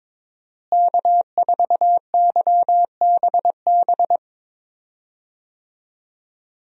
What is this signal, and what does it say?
Morse code, heard as a keyed tone, about 22 wpm, 700 Hz. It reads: K4YBB
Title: USA Call Signs - 5 Characters - Repeated 2x - 22wpm